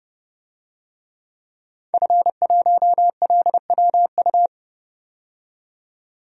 Morse code, heard as a keyed tone, about 29 wpm, 700 Hz. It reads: F1LWU
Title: DXCC-100 Call Signs - 5 Characters - 30wpm